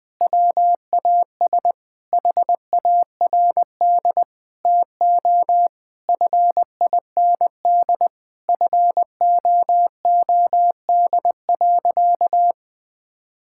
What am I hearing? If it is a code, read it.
Morse code, heard as a keyed tone, about 21 wpm, 700 Hz. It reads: WAS HARD TO FIND FOOD.
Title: Sentences with 3-Word Phrases - 20wpm